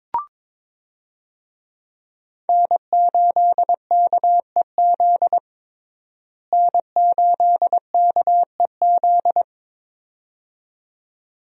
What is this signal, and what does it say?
Morse code, heard as a keyed tone, about 22 wpm, 700 Hz. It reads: N8KEZ N8KEZ
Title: USA Call Signs - 5 Characters - Repeated 2x - 22wpm